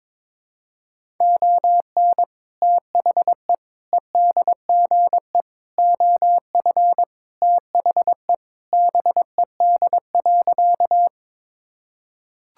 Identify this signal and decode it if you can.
Morse code, heard as a keyed tone, about 22 wpm, 700 Hz. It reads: ON THE EDGE OF THE BED.
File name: practice / 22wpm / Sentences with 3-Word Phrases - 22wpm